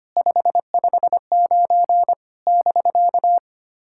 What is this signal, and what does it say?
Morse code, heard as a keyed tone, about 25 wpm, 700 Hz. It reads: 559 <BK>
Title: SOTA QSO Practice 25wpm